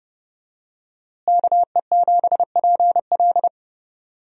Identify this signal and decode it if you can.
Morse code, heard as a keyed tone, about 32 wpm, 700 Hz. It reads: KE7PL
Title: USA Call Signs - 5 Characters - Repeated 2x - 30wpm